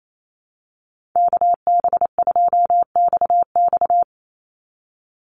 Morse code, heard as a keyed tone, about 28 wpm, 700 Hz. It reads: KB2XX
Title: NA SOTA Chaser Callsigns 28wpm